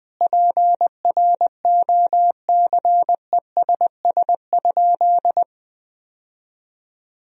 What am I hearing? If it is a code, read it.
Morse code, heard as a keyed tone, about 20 wpm, 700 Hz. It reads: PROCESS?
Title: Sentences from Top 1000 Words - Review Part 3 - 20wpm